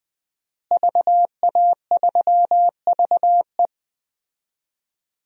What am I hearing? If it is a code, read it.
Morse code, heard as a keyed tone, about 21 wpm, 700 Hz. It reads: VA3VE
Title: DXCC-100 Call Signs - 5 Characters - Repeated 3x - 20wpm